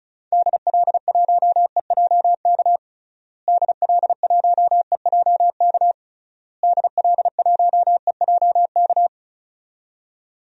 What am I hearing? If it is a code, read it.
Morse code, heard as a keyed tone, about 36 wpm, 700 Hz. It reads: DL1EJK DL1EJK DL1EJK
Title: DXCC-100 Call Signs - 6 Characters - Repeated 3x - 35wpm